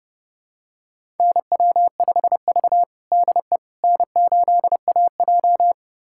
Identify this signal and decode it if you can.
Morse code, heard as a keyed tone, about 31 wpm, 700 Hz. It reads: NW5V DE N8AJ
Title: USA Callsign DE Callsign - 4 Characters - 30wpm